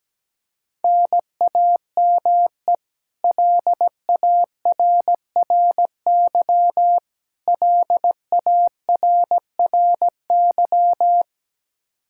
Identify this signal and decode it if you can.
Morse code, heard as a keyed tone, about 17 wpm, 700 Hz. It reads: NAME LARRY LARRY
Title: Practice QSO - Exchange 1 - Beginning - 17wpm